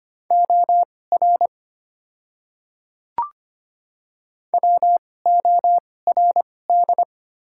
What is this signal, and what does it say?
Morse code, heard as a keyed tone, about 25 wpm, spaced as for 18 wpm, 690 Hz. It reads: OR WORD
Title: 2x Character Spacing - Top 200 Words - 25wpm